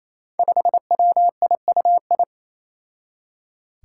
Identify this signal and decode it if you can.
Morse code, heard as a keyed tone, about 28 wpm, 700 Hz. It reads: 5WIUI
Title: Five Challenging Letter-Number Combinations 28wpm